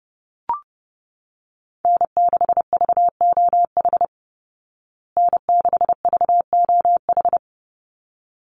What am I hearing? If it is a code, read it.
Morse code, heard as a keyed tone, about 30 wpm, 700 Hz. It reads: N6VOH N6VOH
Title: Call Signs Repeated 3x - 30wpm